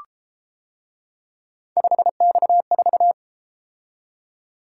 Morse code, heard as a keyed tone, about 33 wpm, 700 Hz. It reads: 5X4